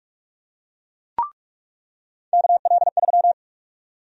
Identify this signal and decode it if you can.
Morse code, heard as a keyed tone, about 44 wpm, 700 Hz. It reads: KL3
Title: USA Call Sign Prefixes - 45wpm